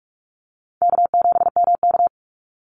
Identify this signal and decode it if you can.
Morse code, heard as a keyed tone, about 44 wpm, 700 Hz. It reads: K7MK
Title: NA SOTA Activator Callsigns 45wpm